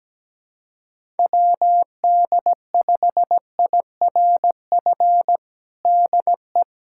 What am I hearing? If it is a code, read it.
Morse code, heard as a keyed tone, about 17 wpm, 700 Hz. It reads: WD5IRF DE
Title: USA Callsign DE Callsign - 6 Characters - 17wpm